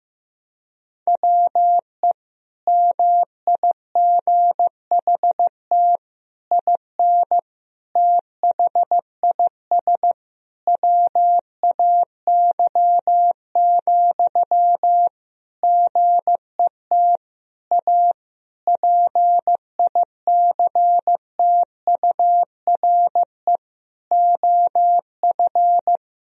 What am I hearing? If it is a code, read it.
Morse code, heard as a keyed tone, about 16 wpm, 700 Hz. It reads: WE MIGHT IN THIS WAY, GET A PICTURE OF
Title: Sentences from Top 1000 Words - Review Part 1 - 15wpm